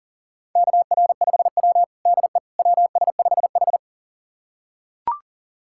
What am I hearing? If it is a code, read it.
Morse code, heard as a keyed tone, about 40 wpm, 700 Hz. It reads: KR5W DE WS5H
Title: USA Callsign DE Callsign - 4 Characters - 40wpm